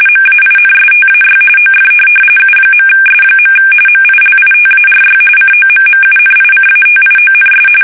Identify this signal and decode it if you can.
RTTY, about 45.45 baud. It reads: MAINSTAY OF U.S. MILITARY COMMUNICATIONS
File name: fsk850 / teletype-corp